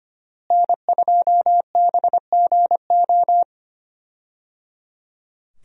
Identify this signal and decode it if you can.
Morse code, heard as a keyed tone, about 25 wpm, 700 Hz. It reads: N2BGO
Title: USA Call Signs - 5 Characters - Repeated 3x - 25wpm